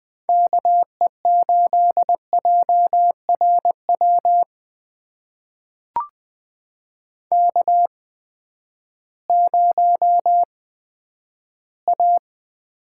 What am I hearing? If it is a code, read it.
Morse code, heard as a keyed tone, about 20 wpm, spaced as for 11 wpm, 700 Hz. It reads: KE8JRW K0A
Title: USA Call Signs - 8x Character Spacing - 20wpm